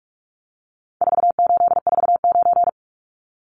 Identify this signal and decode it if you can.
Morse code, heard as a keyed tone, about 44 wpm, 700 Hz. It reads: 4849
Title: Numbers - 4-Digit 45wpm